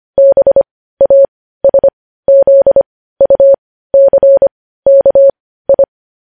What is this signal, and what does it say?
Morse code, heard as a keyed tone, about 25 wpm, spaced as for 15 wpm, 550 Hz. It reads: BASZUCKI
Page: news-headlines-in-morse-code-at-15-wpm